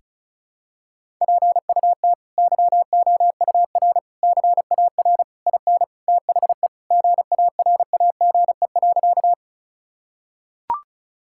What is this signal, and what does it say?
Morse code, heard as a keyed tone, about 36 wpm, 700 Hz. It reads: PUT YOUR CAR IN THE GARAGE.
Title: Sentences from Top 1600 Words 35wpm